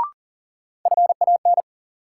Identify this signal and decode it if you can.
Morse code, heard as a keyed tone, about 40 wpm, 700 Hz. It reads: FAN
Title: Sentences from Top 1000 Words - Review Part 4 - New Word Drill - 40wpm